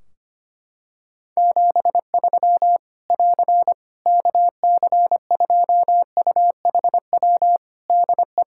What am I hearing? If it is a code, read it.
Morse code, heard as a keyed tone, about 25 wpm, 700 Hz. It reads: 73 <AR> KC2UHW DE
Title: Practice QSO - Exchange 4 - Ending - 25wpm